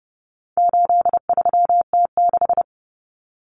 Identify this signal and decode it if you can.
Morse code, heard as a keyed tone, about 30 wpm, 700 Hz. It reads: 83T6
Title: Common Cut Numbers - 4-Digit 30wpm